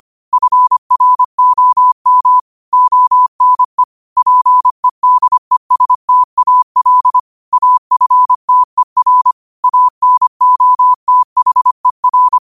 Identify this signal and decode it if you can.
Morse code, heard as a keyed tone, about 25 wpm, 1000 Hz. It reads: FROM ONE PEDESTAL AFTER ANOTHER
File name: qotdfiles / 25wpm